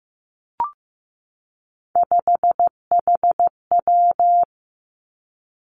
Morse code, heard as a keyed tone, about 15 wpm, 700 Hz. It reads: 5HW
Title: Three Challenging Letter-Number Combinations 15wpm